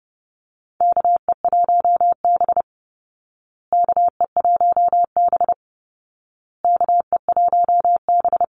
Morse code, heard as a keyed tone, about 30 wpm, 700 Hz. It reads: KE1B KE1B KE1B
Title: Call Signs Repeated 2x - 30wpm